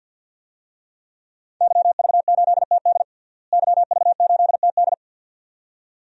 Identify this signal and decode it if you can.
Morse code, heard as a keyed tone, about 53 wpm, 700 Hz. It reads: YV8TD YV8TD
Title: DXCC-100 Call Signs - 5 Characters - Repeated 2x - 50wpm